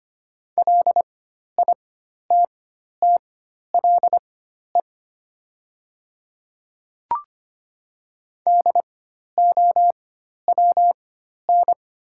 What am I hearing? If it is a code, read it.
Morse code, heard as a keyed tone, about 25 wpm, spaced as for 12 wpm, 700 Hz. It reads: LITTLE DOWN
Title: 4x Character Spacing - Top 300 Words - 25wpm